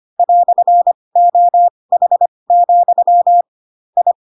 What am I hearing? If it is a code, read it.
Morse code, heard as a keyed tone, about 25 wpm, spaced as for 20 wpm, 700 Hz. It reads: "OH, I
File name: Pinocchio / Pinocchio ch1 15 25 ik1hge